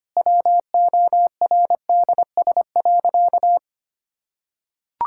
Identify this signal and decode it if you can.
Morse code, heard as a keyed tone, about 25 wpm, 700 Hz. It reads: WORDS.
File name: Easy Sentences with Hints - 25wpm